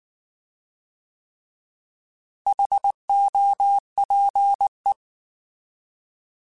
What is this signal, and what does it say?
Morse code, heard as a keyed tone, about 19 wpm, 800 Hz. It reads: HOPE